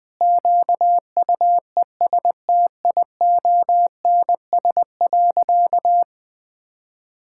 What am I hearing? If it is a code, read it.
Morse code, heard as a keyed tone, about 20 wpm, 700 Hz. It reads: QUESTIONS.
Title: Sentences with 4-Word Phrases - Spoken Beginning - 20wpm